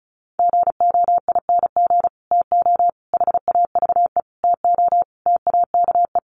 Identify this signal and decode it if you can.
Morse code, heard as a keyed tone, about 34 wpm, 700 Hz. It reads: GOING TO HAVE TO TAKE
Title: Sets of 5 Words Encore 35wpm